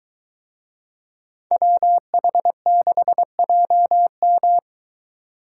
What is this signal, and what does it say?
Morse code, heard as a keyed tone, about 23 wpm, 700 Hz. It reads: WH6JM